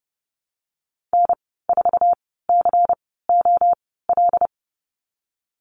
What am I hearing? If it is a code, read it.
Morse code, heard as a keyed tone, about 32 wpm, spaced as for 17 wpm, 700 Hz. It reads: N4COL
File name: Lesson 19g - Letter w - Call Signs 30wpm